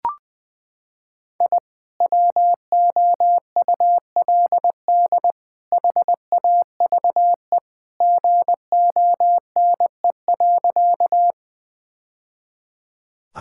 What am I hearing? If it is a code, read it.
Morse code, heard as a keyed tone, about 20 wpm, 700 Hz. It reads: I WOULD HAVE GONE.
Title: Sentences with 3-Word Phrases - 20wpm